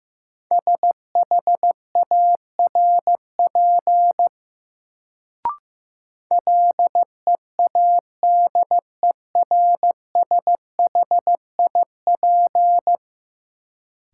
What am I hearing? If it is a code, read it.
Morse code, heard as a keyed tone, about 15 wpm, 700 Hz. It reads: SHARP LEADERSHIP
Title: Sentences from Top 1600 Words - New Word Drill - 15wpm